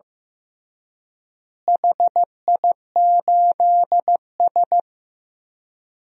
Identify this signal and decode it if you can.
Morse code, heard as a keyed tone, about 15 wpm, 700 Hz. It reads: HI8S
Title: DXCC-100 Call Signs - 4 Characters - Repeated 3x - 15wpm